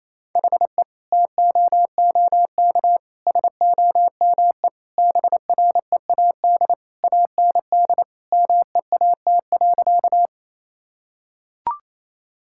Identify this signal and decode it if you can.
Morse code, heard as a keyed tone, about 28 wpm, 700 Hz. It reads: HE TOOK SOME BREAD AND MEAT.
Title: Sentences from Top 1600 Words 28wpm